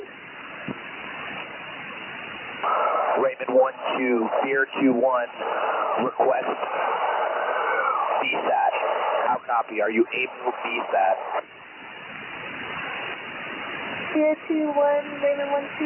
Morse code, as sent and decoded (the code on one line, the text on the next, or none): none